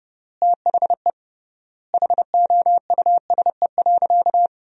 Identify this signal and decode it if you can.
Morse code, heard as a keyed tone, about 30 wpm, 700 Hz. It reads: THE HOUSE.
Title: 3x Word-Spacing - Sentences from Top 500 Words - 30wpm